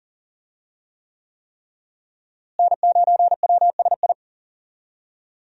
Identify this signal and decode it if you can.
Morse code, heard as a keyed tone, about 40 wpm, 700 Hz. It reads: N9WSI
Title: USA Call Signs - 5 Characters - Repeated 3x - 40wpm